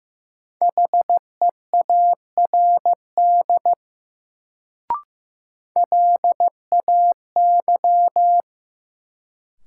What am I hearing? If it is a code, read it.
Morse code, heard as a keyed tone, about 16 wpm, 700 Hz. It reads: HEARD LAY
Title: Sentences from Top 400 Words - New Word Drill - 15wpm